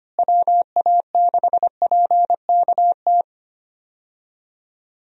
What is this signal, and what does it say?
Morse code, heard as a keyed tone, about 25 wpm, 700 Hz. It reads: WA6PKT
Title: USA Call Signs - 6 Characters - Repeated 2x - 25wpm